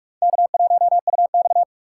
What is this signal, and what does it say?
Morse code, heard as a keyed tone, about 39 wpm, 700 Hz. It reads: K1UX